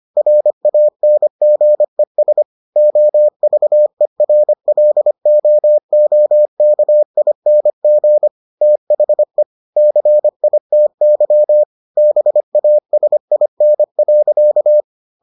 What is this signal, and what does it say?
Morse code, heard as a keyed tone, about 25 wpm, 600 Hz. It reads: RANGES OVERLOOKING THE CITY BASIN.